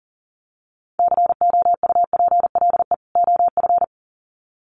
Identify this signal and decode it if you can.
Morse code, heard as a keyed tone, about 40 wpm, 700 Hz. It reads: COUPLE OF
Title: Sets of 2 Words 40wpm